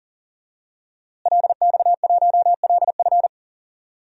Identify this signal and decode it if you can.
Morse code, heard as a keyed tone, about 37 wpm, 700 Hz. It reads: LX1LF
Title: DXCC-100 Call Signs - 5 Characters - Repeated 3x - 40wpm